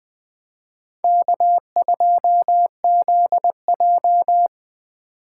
Morse code, heard as a keyed tone, about 24 wpm, 700 Hz. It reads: K2ZJ
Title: Speed Racing - USA 1x2 Call Signs 20wpm